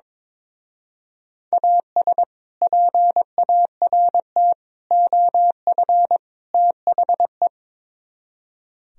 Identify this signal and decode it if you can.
Morse code, heard as a keyed tone, about 22 wpm, 700 Hz. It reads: AS PART OF THE
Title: Speed-Racing - Sets of 4 Words 22wpm